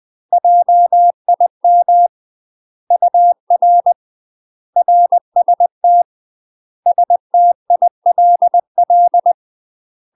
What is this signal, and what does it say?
Morse code, heard as a keyed tone, about 20 wpm, 700 Hz. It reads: JIM UR RST STILL
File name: UK MASTER QSOs TRAINING / 20WPM / UK MASTER QSOs 0030